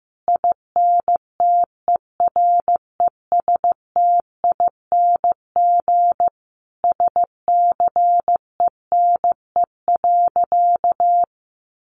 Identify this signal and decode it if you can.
Morse code, heard as a keyed tone, about 15 wpm, 700 Hz. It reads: INTERESTING SCENE.
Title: Sentences from Top 1100 Words 15wpm